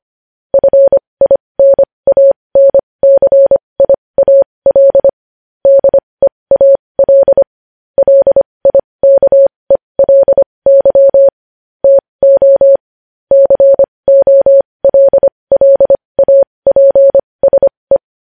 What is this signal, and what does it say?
Morse code, heard as a keyed tone, about 25 wpm, spaced as for 20 wpm, 550 Hz. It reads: FINANCIAL DEAL LIKELY TO COLLAPSE
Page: News Headlines in Morse Code at 20 WPM